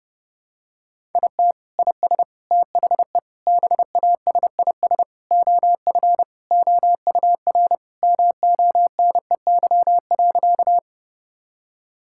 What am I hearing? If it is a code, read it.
Morse code, heard as a keyed tone, about 30 wpm, 700 Hz. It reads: IT IS THE BASIS OF OUR MONEY.
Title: Sentences from Top 1000 Words - Review Part 5 - 30wpm